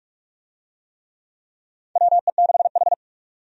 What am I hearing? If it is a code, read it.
Morse code, heard as a keyed tone, about 47 wpm, 700 Hz. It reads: WE6H
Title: USA Call Signs - 4 Characters - 45wpm